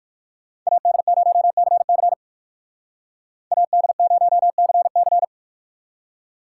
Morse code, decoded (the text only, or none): AD0CC AD0CC